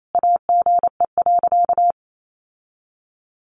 Morse code, heard as a keyed tone, about 28 wpm, 700 Hz. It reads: AGE.
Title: Sentences from Top 1300 Words 28wpm